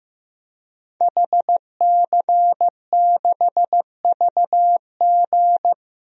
Morse code, heard as a keyed tone, about 16 wpm, 700 Hz. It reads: HC6VG
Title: DXCC-100 Call Signs - 5 Characters - Repeated 3x - 15wpm